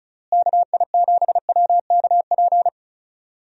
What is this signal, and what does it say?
Morse code, heard as a keyed tone, about 34 wpm, 700 Hz. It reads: KI7WKP
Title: USA Call Signs - 6 Characters - Repeated 2x - 35wpm